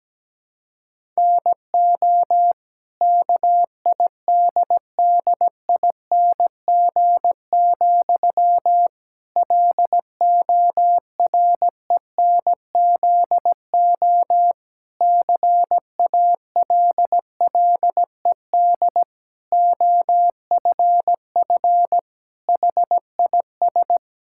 Morse code, heard as a keyed tone, about 17 wpm, 700 Hz. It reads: NO KIDDING, LORENZO CALLED OFF HIS
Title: Pangram Challenge v2 - 17wpm